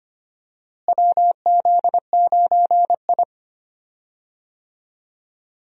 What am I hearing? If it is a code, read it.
Morse code, heard as a keyed tone, about 25 wpm, 700 Hz. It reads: WZ9I
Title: USA Call Signs - 4 Characters - Repeated 3x - 25wpm